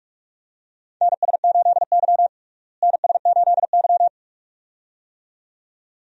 Morse code, heard as a keyed tone, about 46 wpm, 700 Hz. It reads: NS8Y NS8Y
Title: USA Call Signs - 4 Characters - Repeated 2x - 45wpm